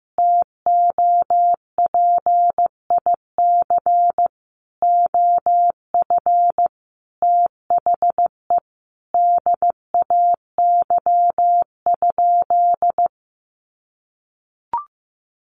Morse code, heard as a keyed tone, about 15 wpm, 700 Hz. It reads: TOPIC OF THE DAY?
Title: Sentences from Top 1200 Words 15wpm